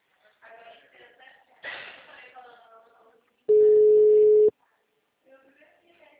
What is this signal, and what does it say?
Ringback tone (Europe, 425 Hz), with speech.
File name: out- -2003- - - .25365